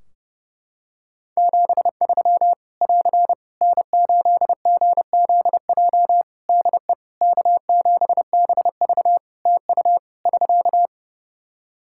Morse code, heard as a keyed tone, about 30 wpm, 700 Hz. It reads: 73 <AR> N8GZJ DE K7BV TU <SK>
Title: Practice QSO - Exchange 4 - Ending - 30wpm